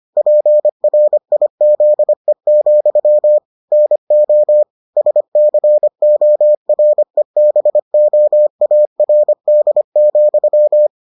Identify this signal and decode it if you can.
Morse code, heard as a keyed tone, about 25 wpm, 600 Hz. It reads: PRIZE, NO SCOREBOARD,